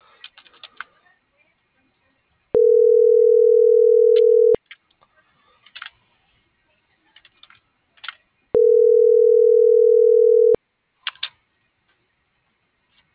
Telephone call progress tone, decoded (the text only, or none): ringback tone